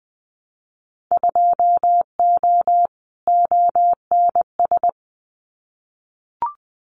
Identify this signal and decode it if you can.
Morse code, heard as a keyed tone, about 20 wpm, 700 Hz. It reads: 2O ONS
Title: Winter Field Day Exchanges Encore 20wpm